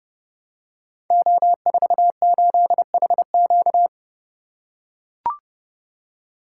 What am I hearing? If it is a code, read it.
Morse code, heard as a keyed tone, about 30 wpm, 700 Hz. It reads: O48HQ
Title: Code Groups - 5 Letter:Numbers - 30wpm